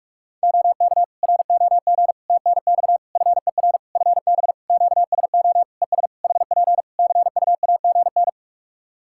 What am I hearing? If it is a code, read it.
Morse code, heard as a keyed tone, about 46 wpm, 700 Hz. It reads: OK ROG TNX FER FB QSO ES HP CUAGN
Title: Practice QSO - Exchange 4 - Ending - 45wpm